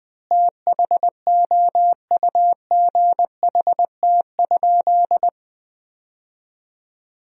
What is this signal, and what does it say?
Morse code, heard as a keyed tone, about 20 wpm, 700 Hz. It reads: THOUGHT?
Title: 4x Word-Spacing - Sentences from Top 300 Words - 20wpm